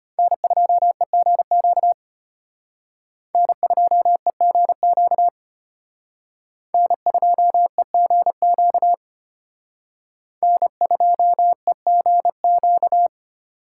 Morse code, heard as a keyed tone, about 32 wpm, 700 Hz. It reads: N2EGQ N2EGQ N2EGQ N2EGQ